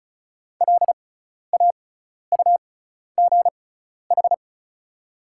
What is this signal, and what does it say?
Morse code, heard as a keyed tone, about 36 wpm, spaced as for 12 wpm, 700 Hz. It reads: LAUGH
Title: 6x Character Spacing - Top 500 Words - 35wpm